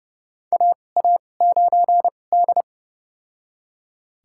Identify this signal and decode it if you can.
Morse code, heard as a keyed tone, about 30 wpm, spaced as for 22 wpm, 700 Hz. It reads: AA9D
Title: USA Call Signs - 4 Characters - 2x Character Spacing - 30wpm